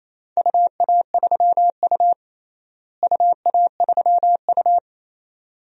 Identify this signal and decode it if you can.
Morse code, heard as a keyed tone, about 28 wpm, 700 Hz. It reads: UA3U UA3U
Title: DXCC-100 Call Signs - 4 Characters - Repeated 2x - 28wpm